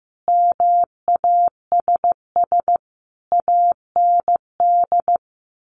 Morse code, heard as a keyed tone, about 14 wpm, 700 Hz. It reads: MASS AND
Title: A Sound Of Thunder 15wpm